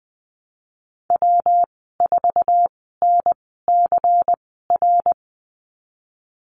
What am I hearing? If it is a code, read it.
Morse code, heard as a keyed tone, about 20 wpm, spaced as for 15 wpm, 700 Hz. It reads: W4NCR
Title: 154 Lesson 25g - Letter p - Call Signs 20wpm